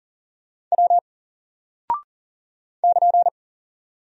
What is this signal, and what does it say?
Morse code, heard as a keyed tone, about 38 wpm, 700 Hz. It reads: W <KN>
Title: Single Character (Letter:Number:Punctuation:Prosign) 40wpm